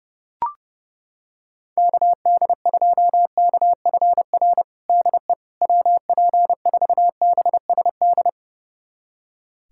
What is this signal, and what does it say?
Morse code, heard as a keyed tone, about 30 wpm, 700 Hz. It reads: KD2KFR DE WP4BSD
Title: USA Callsign DE Callsign 30wpm